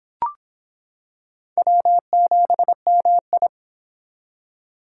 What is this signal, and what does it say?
Morse code, heard as a keyed tone, about 25 wpm, 700 Hz. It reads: W7MI